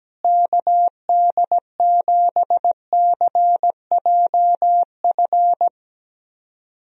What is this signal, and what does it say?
Morse code, heard as a keyed tone, about 17 wpm, 700 Hz. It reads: KD7CJF
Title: USA Call Signs - 6 Characters - Repeated 3x - 17wpm